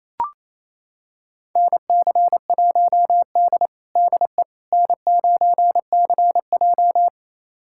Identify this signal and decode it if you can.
Morse code, heard as a keyed tone, about 27 wpm, 700 Hz. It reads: NC1D DE N9CJ
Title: USA Callsign DE Callsign - 4 Characters - 28wpm